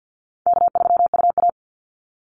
Morse code, heard as a keyed tone, about 50 wpm, 700 Hz. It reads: K3UA
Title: Call Signs - 50wpm